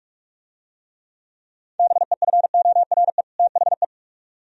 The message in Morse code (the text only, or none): BEFORE THE